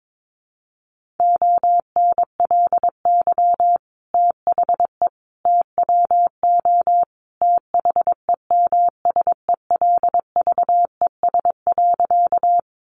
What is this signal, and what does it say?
Morse code, heard as a keyed tone, about 22 wpm, 700 Hz. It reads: ONLY THE TWO THEMSELVES.
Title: Sentences from Top 800 Words 22wpm